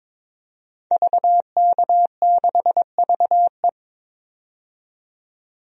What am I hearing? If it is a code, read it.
Morse code, heard as a keyed tone, about 22 wpm, 700 Hz. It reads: VK6VE
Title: DXCC-100 Call Signs - 5 Characters - 22wpm